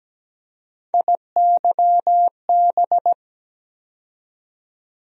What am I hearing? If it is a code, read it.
Morse code, heard as a keyed tone, about 17 wpm, 700 Hz. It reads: IYB
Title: USA Call Sign Postfixes - 3 Letters 17wpm